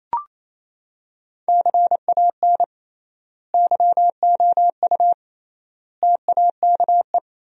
Morse code, heard as a keyed tone, about 28 wpm, 700 Hz. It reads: CAN YOU TAKE
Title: 3x Word-Spacing - Sentences from Top 200 Words - 28wpm